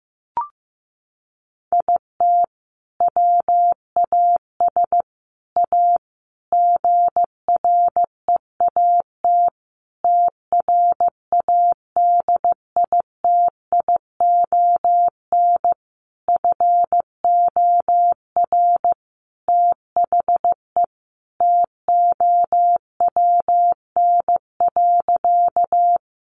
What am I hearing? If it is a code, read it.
Morse code, heard as a keyed tone, about 15 wpm, 700 Hz. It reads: IT WAS A GREAT TRADITION FOR THE TOWN.
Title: Sentences from Top 1500 Words 15wpm